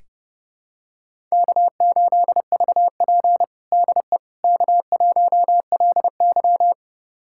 Morse code, heard as a keyed tone, about 30 wpm, 700 Hz. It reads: K8VP DE K1LY
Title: USA Callsign DE Callsign - 4 Characters - 30wpm